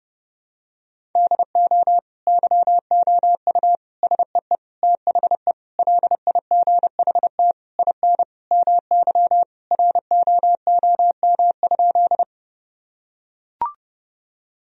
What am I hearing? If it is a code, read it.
Morse code, heard as a keyed tone, about 30 wpm, 700 Hz. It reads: DO YOU SEE THE LIGHT IN MY ROOM?
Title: Sentences from Top 1000 Words - Review Part 1 - 30wpm